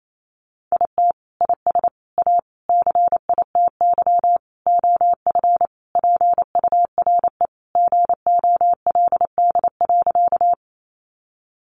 Morse code, heard as a keyed tone, about 28 wpm, 700 Hz. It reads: IT IS A CITY OF PURE GOLD.
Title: Sentences from Top 1500 Words 28wpm